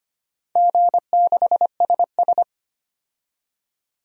Morse code, heard as a keyed tone, about 25 wpm, 700 Hz. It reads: G6SS
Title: DXCC-100 Call Signs - 4 Characters - 25wpm